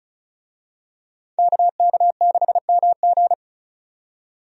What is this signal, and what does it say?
Morse code, heard as a keyed tone, about 36 wpm, 700 Hz. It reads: KK6MG
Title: USA Call Signs - 5 Characters - 35wpm